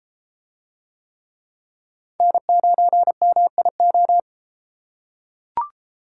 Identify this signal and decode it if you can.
Morse code, heard as a keyed tone, about 32 wpm, 700 Hz. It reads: N9MIO